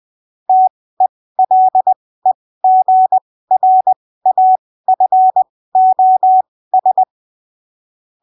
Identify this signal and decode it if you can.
Morse code, heard as a keyed tone, about 20 wpm, spaced as for 15 wpm, 750 Hz. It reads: TELEGRAFOS
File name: arquivos / texto / texto 14 20ppm